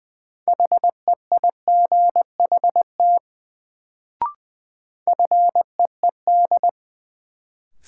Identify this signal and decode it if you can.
Morse code, heard as a keyed tone, about 21 wpm, 700 Hz. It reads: HEIGHT FEED
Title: Sentences from Top 1600 Words - New Word Drill Alt - 20wpm